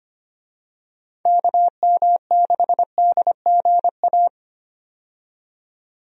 Morse code, heard as a keyed tone, about 25 wpm, 700 Hz. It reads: KM6DGA
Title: USA Call Signs - 6 Characters - 25wpm